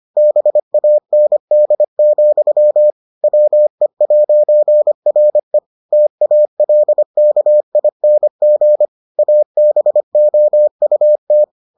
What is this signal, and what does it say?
Morse code, heard as a keyed tone, about 25 wpm, 600 Hz. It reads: BAND, WE'RE TALKING ABOUT